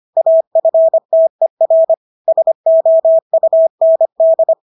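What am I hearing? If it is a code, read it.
Morse code, heard as a keyed tone, about 25 wpm, 650 Hz. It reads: AFTER SOUND
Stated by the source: Morse practice of English words at 25 wpm from CW med Gnister.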